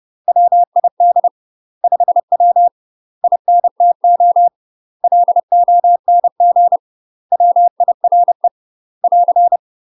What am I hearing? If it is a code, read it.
Morse code, heard as a keyed tone, about 30 wpm, 700 Hz. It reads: WID 5W INTO LONG WIRE <AR>
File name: UK MASTER QSOs TRAINING / 30WPM / UK MASTER QSOs 0044